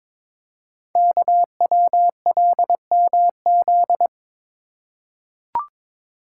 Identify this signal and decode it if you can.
Morse code, heard as a keyed tone, about 22 wpm, 700 Hz. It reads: KWLMZ
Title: Code Groups - 5 Letter:Numbers - 22wpm